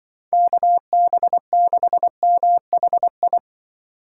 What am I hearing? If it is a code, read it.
Morse code, heard as a keyed tone, about 24 wpm, 700 Hz. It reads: KB6MHI
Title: Speed Racing - USA 2x3 Call Signs 20wpm